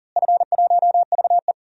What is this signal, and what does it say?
Morse code, heard as a keyed tone, about 40 wpm, 700 Hz. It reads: F1VE
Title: DXCC-100 Call Signs - 4 Characters - Repeated 2x - 40wpm